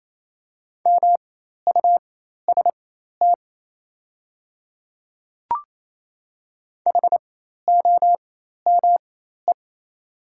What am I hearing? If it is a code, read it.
Morse code, heard as a keyed tone, about 28 wpm, spaced as for 13 wpm, 700 Hz. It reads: MUST HOME
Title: 4x Character Spacing - Top 200 Words - 28wpm